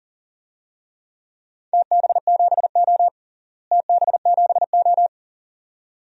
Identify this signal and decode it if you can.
Morse code, heard as a keyed tone, about 40 wpm, 690 Hz. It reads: TB7O TB7O
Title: DXCC-100 Call Signs - 4 Characters - Repeated 2x - 40wpm